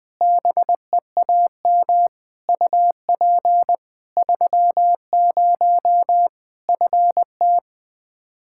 Morse code, heard as a keyed tone, about 20 wpm, 700 Hz. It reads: BEAM UP 30 FT
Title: Practice QSO - Exchange 2 - Info - 20wpm